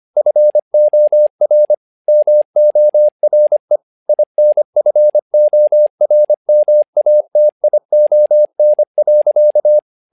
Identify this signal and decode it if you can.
Morse code, heard as a keyed tone, about 25 wpm, 600 Hz. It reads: FOR MORE INFORMATION.
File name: learning center / tower trouble / tower trouble 25wpm